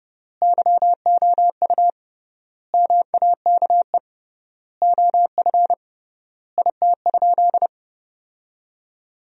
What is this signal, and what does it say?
Morse code, heard as a keyed tone, about 32 wpm, 700 Hz. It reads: YOU MAKE OF IT?
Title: 3x Word-Spacing - Sentences from Top 100 Words - 30wpm